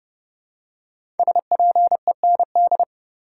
Morse code, heard as a keyed tone, about 30 wpm, 700 Hz. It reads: SPEND
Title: Five Letter Words - 30wpm